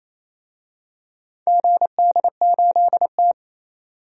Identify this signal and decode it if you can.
Morse code, heard as a keyed tone, about 28 wpm, 700 Hz. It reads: GD8T
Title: DXCC-100 Call Signs - 4 Characters - 28wpm